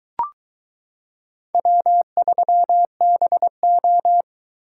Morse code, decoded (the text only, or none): W3BO